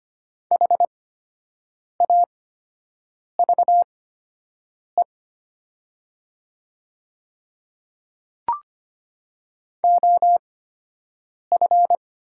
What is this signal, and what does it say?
Morse code, heard as a keyed tone, about 25 wpm, spaced as for 7 wpm, 700 Hz. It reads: HAVE OF
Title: 8x Character Spacing - Top 100 Words - 25wpm